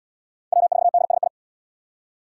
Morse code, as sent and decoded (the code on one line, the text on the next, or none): .-- ....- -.. ... ..
W4DSI